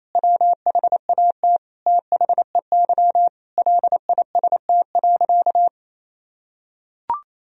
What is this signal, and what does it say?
Morse code, heard as a keyed tone, about 28 wpm, 700 Hz. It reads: WHAT THEY LIST.
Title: Sentences from Top 1000 Words - Review Part 2 - 28wpm